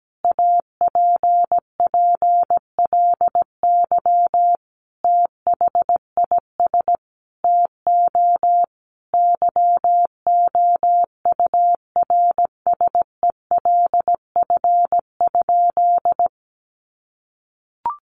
Morse code, heard as a keyed tone, about 17 wpm, 700 Hz. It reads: APPLY THIS TO YOURSELF?
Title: Sentences from Top 1200 Words 17wpm